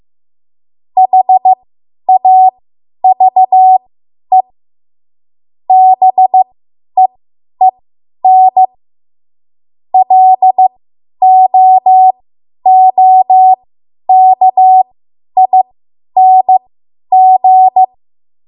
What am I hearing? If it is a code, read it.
Morse code, heard as a keyed tone, about 15 wpm, spaced as for 10 wpm, 750 Hz. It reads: HAVE BEEN LOOKING
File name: Morse / 10WPM